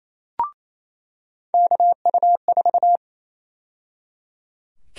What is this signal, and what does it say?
Morse code, heard as a keyed tone, about 28 wpm, 700 Hz. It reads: KU4
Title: USA Call Sign Prefixes - 28wpm